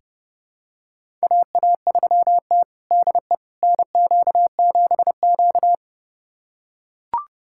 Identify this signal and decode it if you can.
Morse code, heard as a keyed tone, about 30 wpm, 700 Hz. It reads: AA3T DE NQ7Q
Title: USA Callsign DE Callsign - 4 Characters - 30wpm